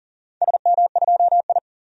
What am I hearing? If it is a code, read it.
Morse code, heard as a keyed tone, about 40 wpm, 700 Hz. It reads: SM2I
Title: DXCC-100 Call Signs - 4 Characters - 40wpm